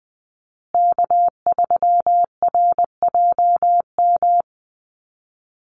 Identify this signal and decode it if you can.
Morse code, heard as a keyed tone, about 20 wpm, 700 Hz. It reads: K3RJM
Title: USA Call Signs Repeated 2x 20wpm